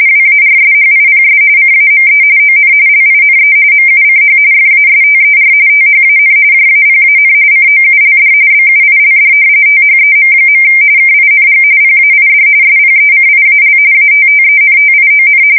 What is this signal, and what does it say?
RTTY, about 45.45 baud. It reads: WARSHIPS CAME ON LINE MUCH FASTER THAN JAPAN COULD BUILD THEM. KING PAID SPECIAL ATTENTION
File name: fsk170 / usnavy4